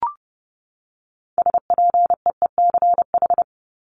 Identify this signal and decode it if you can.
Morse code, heard as a keyed tone, about 30 wpm, 700 Hz. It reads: SPEECH
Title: Top 1751-2000 Words 30wpm